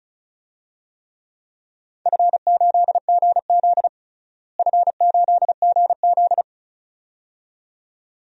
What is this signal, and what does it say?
Morse code, heard as a keyed tone, about 34 wpm, 700 Hz. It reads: F8GZ F8GZ
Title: DXCC-100 Call Signs - 4 Characters - Repeated 2x - 35wpm